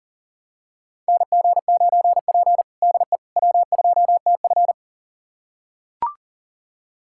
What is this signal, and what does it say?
Morse code, heard as a keyed tone, about 37 wpm, 700 Hz. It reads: NG9P DE W2TF
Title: USA Callsign DE Callsign - 4 Characters - 40wpm